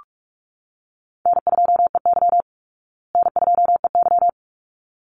Morse code, heard as a keyed tone, about 44 wpm, 700 Hz. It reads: N2EY N2EY
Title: Call Signs Repeated 3x - 45wpm